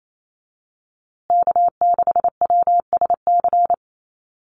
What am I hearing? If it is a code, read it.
Morse code, heard as a keyed tone, about 28 wpm, 700 Hz. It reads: K6WSC
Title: USA Call Signs Repeated 2x 28wpm